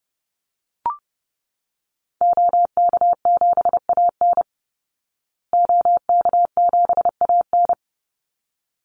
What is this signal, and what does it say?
Morse code, heard as a keyed tone, about 30 wpm, 700 Hz. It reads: OK7AN OK7AN
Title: Call Signs Repeated 3x - 30wpm